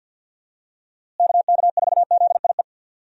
Morse code, heard as a keyed tone, about 52 wpm, 700 Hz. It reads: KK4ZIE
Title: USA Call Signs - 6 Characters - Repeated 3x - 50wpm